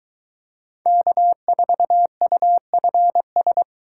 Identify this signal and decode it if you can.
Morse code, heard as a keyed tone, about 21 wpm, 700 Hz. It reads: K4UFS